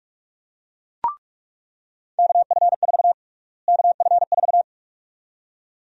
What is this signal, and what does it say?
Morse code, heard as a keyed tone, about 48 wpm, 700 Hz. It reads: KF4 KF4
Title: USA Call Sign Prefixes - Repeated 2x - 45wpm